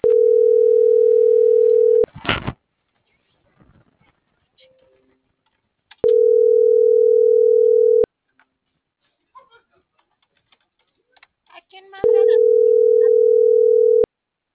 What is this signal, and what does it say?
Ringback tone (North America), 440 + 480 Hz.
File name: CENTRAL TERRESTRE / out claudiacruz - .74396